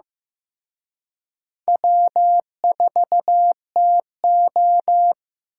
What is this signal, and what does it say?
Morse code, heard as a keyed tone, about 18 wpm, 700 Hz. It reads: W4TO
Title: Speed Racing - USA 1x2 Call Signs 15wpm